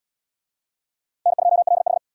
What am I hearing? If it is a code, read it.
Morse code, heard as a keyed tone, about 68 wpm, 700 Hz. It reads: N2LH